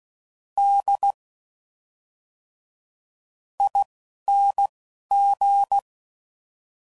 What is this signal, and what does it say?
Morse code, heard as a keyed tone, about 16 wpm, spaced as for 6 wpm, 800 Hz. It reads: DING
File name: Suffixes / Special INGs / Fast INGs